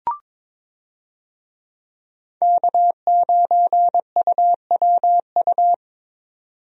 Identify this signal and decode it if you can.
Morse code, heard as a keyed tone, about 21 wpm, 700 Hz. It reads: K9UWU
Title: USA Call Signs - 5 Characters - 22wpm